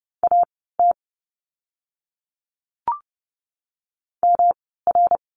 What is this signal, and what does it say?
Morse code, heard as a keyed tone, about 30 wpm, spaced as for 17 wpm, 700 Hz. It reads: AT MR
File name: course / 30:10wpm / 108 Lesson 18f - Letter m - Words 30wpm